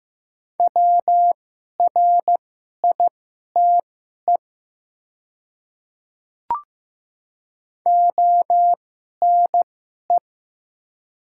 Morse code, heard as a keyed tone, about 15 wpm, spaced as for 11 wpm, 700 Hz. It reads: WRITE ONE
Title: 2x Character Spacing - Top 100 Words - 15wpm